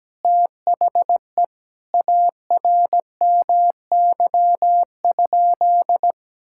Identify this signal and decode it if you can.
Morse code, heard as a keyed tone, about 17 wpm, 700 Hz. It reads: THE ARMY?
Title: Sentences from Top 1000 Words - Review Part 6 - 17wpm